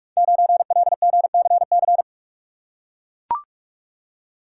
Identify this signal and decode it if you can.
Morse code, heard as a keyed tone, about 44 wpm, 690 Hz. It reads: 9LGCC
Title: Code Groups - 5 Letter:Numbers - 45wpm